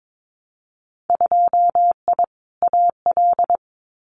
Morse code, heard as a keyed tone, about 22 wpm, 700 Hz. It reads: 2I AL
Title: Winter Field Day Exchanges 22wpm